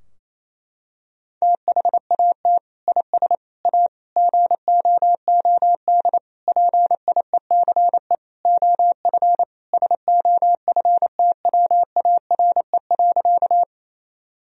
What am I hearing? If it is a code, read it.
Morse code, heard as a keyed tone, about 28 wpm, 700 Hz. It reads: THAT IS A GOOD PIECE OF SOFTWARE.
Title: Sentences from Top 1000 Words - Review Part 3 - 28wpm